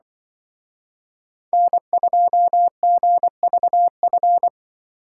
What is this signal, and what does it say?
Morse code, heard as a keyed tone, about 24 wpm, 700 Hz. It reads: N2GVF
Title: Speed Racing - USA 1x3 Call Signs 20wpm